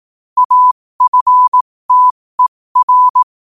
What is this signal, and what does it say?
Morse code, heard as a keyed tone, about 18 wpm, spaced as for 15 wpm, 1000 Hz. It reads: AFTER
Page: New quotes every day in morse code at 15 Words per minute.